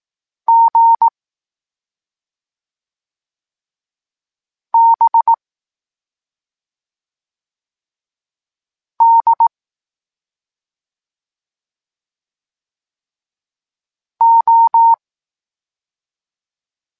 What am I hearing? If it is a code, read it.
Morse code, heard as a keyed tone, about 18 wpm, 920 Hz. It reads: G B D O
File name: Tekst-26 1